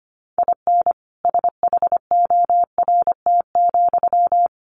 Morse code, heard as a keyed tone, about 25 wpm, 700 Hz. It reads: IN SHORT,
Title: Sentences from Top 400 Words 25wpm